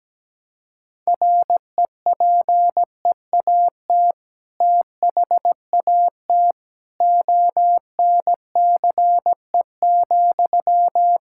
Morse code, heard as a keyed tone, about 17 wpm, 700 Hz. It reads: REPEAT THAT ONCE,
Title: Sentences from Top 1600 Words 17wpm